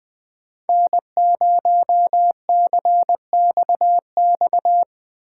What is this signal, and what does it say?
Morse code, heard as a keyed tone, about 20 wpm, 700 Hz. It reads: N0CXX
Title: Speed Racing - USA 1x3 Call Signs 17wpm